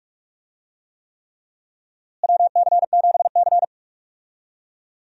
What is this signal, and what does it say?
Morse code, heard as a keyed tone, about 47 wpm, 700 Hz. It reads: WC7C
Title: USA Call Signs - 4 Characters - Repeated 3x - 45wpm